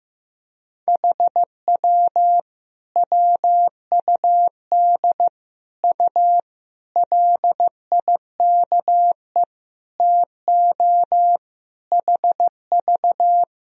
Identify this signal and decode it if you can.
Morse code, heard as a keyed tone, about 16 wpm, 700 Hz. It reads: HW WUD U LIKE TO HV
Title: Sentences from Top 100 Words and Abbreviations 15wpm